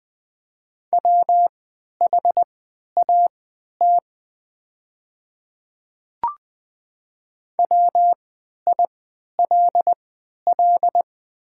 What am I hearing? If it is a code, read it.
Morse code, heard as a keyed tone, about 20 wpm, spaced as for 11 wpm, 700 Hz. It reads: WHAT WILL
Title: 3x Character Spacing - Top 100 Words - 20wpm